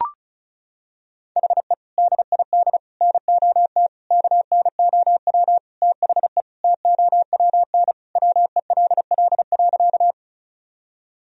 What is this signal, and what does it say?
Morse code, heard as a keyed tone, about 34 wpm, 700 Hz. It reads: HE DID NOT KNOW THE TOWN WELL.
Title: Sentences from Top 1000 Words - Review Part 2 - 35wpm